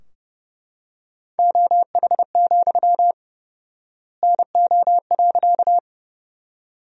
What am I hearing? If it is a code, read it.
Morse code, heard as a keyed tone, about 30 wpm, 700 Hz. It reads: OH, NO.
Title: 4x Word-Spacing - Sentences from Top 500 Words - 30wpm